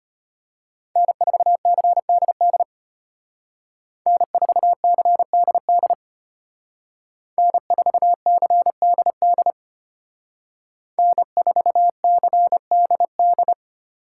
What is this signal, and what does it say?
Morse code, heard as a keyed tone, about 31 wpm, 700 Hz. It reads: N4CDD N4CDD N4CDD N4CDD
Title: Speed Racing - USA 1x3 Call Signs 25wpm